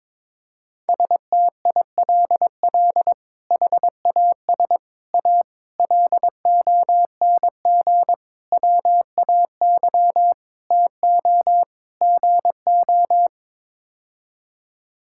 Morse code, heard as a keyed tone, about 22 wpm, 700 Hz. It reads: STILL HAS A LONG WAY TO GO
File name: practice / 22wpm / Sets of 7 Words - 22wpm